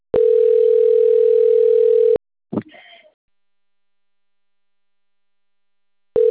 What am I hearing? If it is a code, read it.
Ringback tone (North America), 440 + 480 Hz.